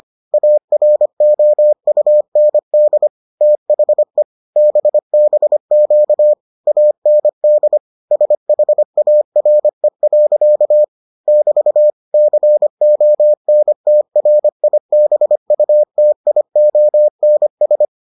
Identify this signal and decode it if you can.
Morse code, heard as a keyed tone, about 25 wpm, 600 Hz. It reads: AROUND THE BBQ AND SHARE. <BT> CONTRIBUTIONS